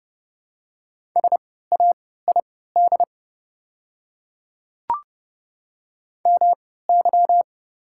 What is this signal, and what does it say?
Morse code, heard as a keyed tone, about 30 wpm, spaced as for 17 wpm, 700 Hz. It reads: SAID MY
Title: 3x Character Spacing - Top 100 Words - 30wpm